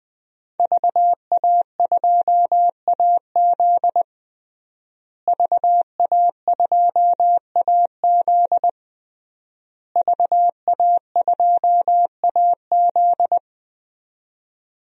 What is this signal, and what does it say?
Morse code, heard as a keyed tone, about 20 wpm, 700 Hz. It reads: VA2AZ VA2AZ VA2AZ
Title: DXCC-100 Call Signs - 5 Characters - Repeated 3x - 20wpm